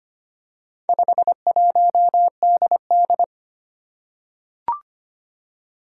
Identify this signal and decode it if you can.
Morse code, heard as a keyed tone, about 25 wpm, 700 Hz. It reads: 51DD
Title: Four Challenging Letter-Number Combinations 25wpm